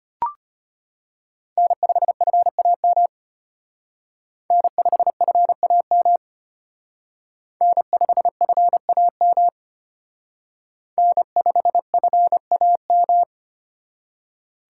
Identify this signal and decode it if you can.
Morse code, heard as a keyed tone, about 32 wpm, 700 Hz. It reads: N5FAM N5FAM N5FAM N5FAM